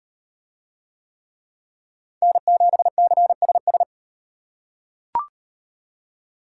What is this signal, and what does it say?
Morse code, heard as a keyed tone, about 38 wpm, 700 Hz. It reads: N7CSS